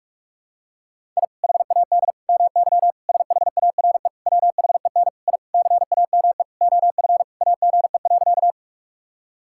Morse code, heard as a keyed tone, about 46 wpm, 700 Hz. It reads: I HAD MY SHARE WHEN I CAME OF AGE.
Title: Sentences from Top 1000 Words - Review Part 6 - 45wpm